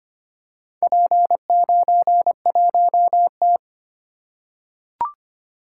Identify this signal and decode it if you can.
Morse code, heard as a keyed tone, about 25 wpm, 690 Hz. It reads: P91T
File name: practice / 25wpm / Four Letter-Number Combinations - 25wpm